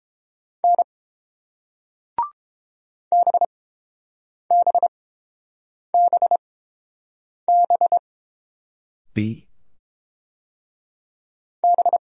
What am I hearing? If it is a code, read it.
Morse code, heard as a keyed tone, about 30 wpm, spaced as for 7 wpm, 690 Hz. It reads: N BBBB B